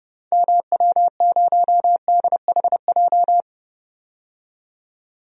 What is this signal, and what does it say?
Morse code, heard as a keyed tone, about 30 wpm, 700 Hz. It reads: MW0DHJ
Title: DXCC-100 Call Signs - 6 Characters - 30wpm